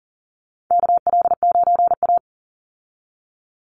Morse code, heard as a keyed tone, about 40 wpm, 700 Hz. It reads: KL9A
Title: Call Signs Repeated 2x - 40wpm